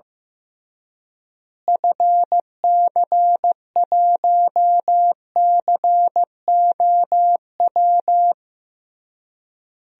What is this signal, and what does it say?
Morse code, heard as a keyed tone, about 14 wpm, 700 Hz. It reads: FC1COW
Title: DXCC-100 Call Signs - 6 Characters - Repeated 3x - 15wpm